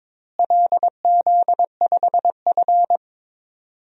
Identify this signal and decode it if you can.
Morse code, heard as a keyed tone, about 22 wpm, 700 Hz. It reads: LZ5F
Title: DXCC-100 Call Signs - 4 Characters - Repeated 3x - 22wpm